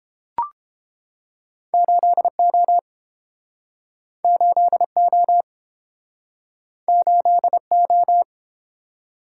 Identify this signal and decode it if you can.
Morse code, heard as a keyed tone, about 32 wpm, 700 Hz. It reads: 8O 8O 8O